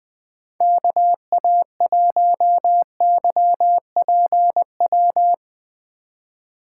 Morse code, heard as a keyed tone, about 23 wpm, 700 Hz. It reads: KA1YPW
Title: Speed Racing - USA 2x3 Call Signs 20wpm